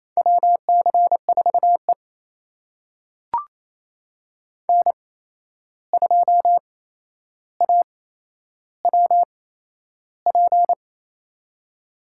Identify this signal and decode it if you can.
Morse code, heard as a keyed tone, about 28 wpm, spaced as for 11 wpm, 700 Hz. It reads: WC4E N2AWP
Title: USA Call Signs - 8x Character Spacing - 28wpm